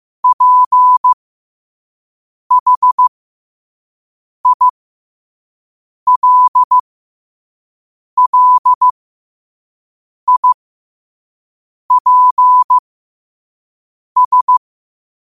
Morse code, heard as a keyed tone, about 15 wpm, spaced as for 5 wpm, 1000 Hz. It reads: PHILLIPS
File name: qotdfiles / 5wpm